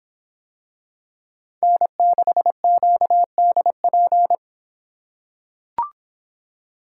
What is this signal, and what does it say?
Morse code, heard as a keyed tone, about 24 wpm, 700 Hz. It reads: N6QDP